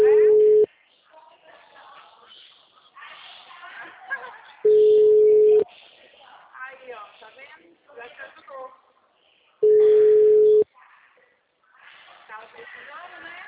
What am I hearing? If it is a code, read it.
Ringback tone (Europe, 425 Hz), with speech.